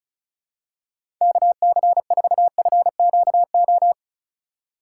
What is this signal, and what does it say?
Morse code, heard as a keyed tone, about 34 wpm, 700 Hz. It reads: KC4FQO
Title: USA Call Signs - 6 Characters - 2x Character Spacing - 35wpm